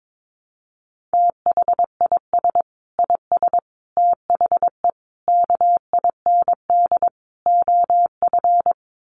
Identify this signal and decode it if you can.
Morse code, heard as a keyed tone, about 22 wpm, 700 Hz. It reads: THIS IS THE KIND OF
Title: Sets of 5 Words Encore 22wpm